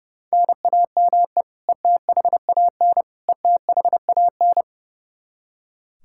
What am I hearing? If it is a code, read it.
Morse code, heard as a keyed tone, about 30 wpm, 700 Hz. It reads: NAME ETHAN ETHAN
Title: Practice QSO - Exchange 1 - Beginning - 30wpm